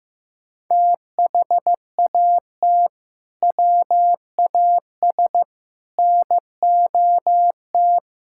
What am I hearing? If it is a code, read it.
Morse code, heard as a keyed tone, about 16 wpm, 700 Hz. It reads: THAT WAS NOT
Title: Sentences from Top 1000 Words - Review Part 5 - 15wpm